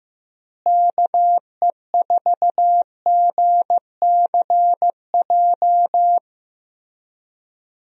Morse code, heard as a keyed tone, about 15 wpm, 700 Hz. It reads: KE4GCJ
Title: USA Call Signs - 6 Characters - Repeated 3x - 15wpm